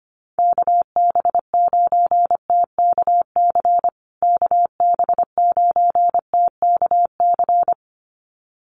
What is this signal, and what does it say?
Morse code, heard as a keyed tone, about 25 wpm, 700 Hz. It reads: KB9TKC KB9TKC
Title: USA Call Signs Repeated 2x 25wpm